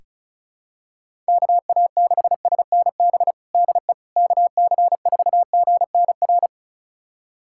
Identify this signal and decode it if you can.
Morse code, heard as a keyed tone, about 34 wpm, 700 Hz. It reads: KA6SNB DE KC4GNR
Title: USA Callsign DE Callsign - 6 Characters - 35wpm